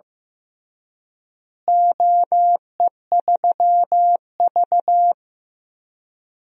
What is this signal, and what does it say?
Morse code, heard as a keyed tone, about 15 wpm, 700 Hz. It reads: OE3V
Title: DXCC-100 Call Signs - 4 Characters - Repeated 3x - 15wpm